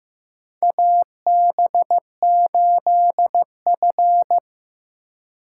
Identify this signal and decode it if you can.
Morse code, heard as a keyed tone, about 15 wpm, 700 Hz. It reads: AB8F
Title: USA Call Signs - 4 Characters - 4x Character Spacing - 15wpm